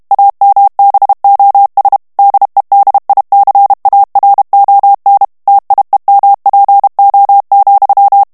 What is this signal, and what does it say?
Morse code, heard as a keyed tone, about 32 wpm, 800 Hz. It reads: AMBOS DEDICARON TIEMPO,
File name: morse mp3 / ESPAGNOL TEXTE EN CLAIR 1 VITESSE 30 MOTS MINUTES